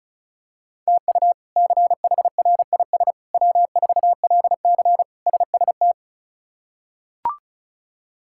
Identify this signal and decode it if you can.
Morse code, heard as a keyed tone, about 36 wpm, 700 Hz. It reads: TU CHRIS W4LC SST
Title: K1USN SST QSO Practice - 35wpm